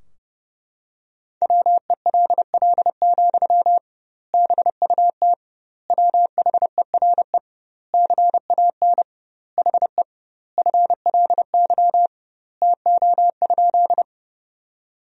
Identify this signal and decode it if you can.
Morse code, heard as a keyed tone, about 30 wpm, 700 Hz. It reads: WELL, BUT WHERE CAN HE FLY TO?
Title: 2x Word-Spacing - Sentences from Top 500 Words - 30wpm